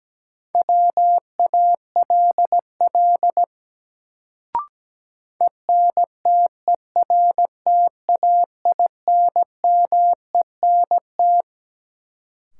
Morse code, heard as a keyed tone, about 17 wpm, 700 Hz. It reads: WALL ENTERTAINMENT
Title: Sentences from Top 1000 Words - Review Part 6 - New Word Drill - 17wpm